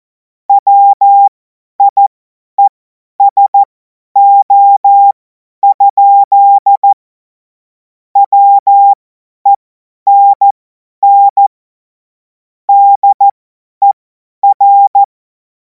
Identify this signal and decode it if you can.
Morse code, heard as a keyed tone, about 14 wpm, spaced as for 10 wpm, 800 Hz. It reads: WIESO? WENN DER
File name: Morsekurs / Witze / Witz 23. Juni 2026 50 F70